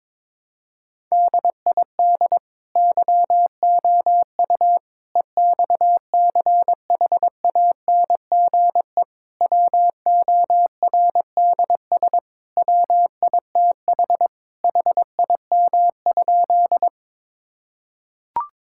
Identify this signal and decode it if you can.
Morse code, heard as a keyed tone, about 22 wpm, 700 Hz. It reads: DID YOU EXCHANGE WORDS WITH HIM?
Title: Sentences from Top 1000 Words - Review Part 6 - 22wpm